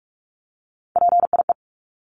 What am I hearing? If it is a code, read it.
Morse code, heard as a keyed tone, about 44 wpm, 700 Hz. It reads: PIE
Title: Top 1751-2000 Words 45wpm